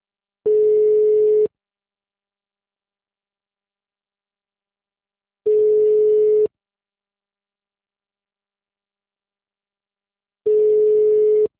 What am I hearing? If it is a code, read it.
Ringback tone (Europe, 425 Hz).